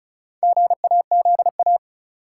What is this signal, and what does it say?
Morse code, heard as a keyed tone, about 36 wpm, 700 Hz. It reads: GAZA
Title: Sets of 5 Words - New Word Drill - 35wpm